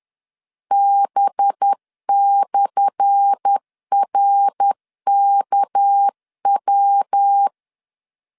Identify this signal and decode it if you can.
Morse code, heard as a keyed tone, about 10 wpm, 800 Hz. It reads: B/RKW
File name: LEZIONI / 10 SIGNS / 10A SLASH 10wpm 2808